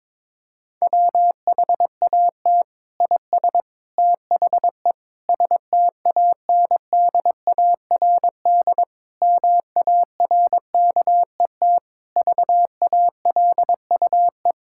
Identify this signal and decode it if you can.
Morse code, heard as a keyed tone, about 23 wpm, 700 Hz. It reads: WHAT IS THE STANDARD MARKET VALUE
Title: Sentences from Top 1000 Words - Review Part 6 - 22wpm